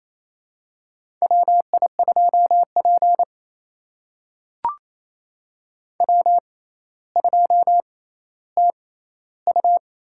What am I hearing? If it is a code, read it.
Morse code, heard as a keyed tone, about 28 wpm, spaced as for 14 wpm, 700 Hz. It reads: WI2P W2TU
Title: USA Call Signs - 4 Characters - 6x Character Spacing - 28wpm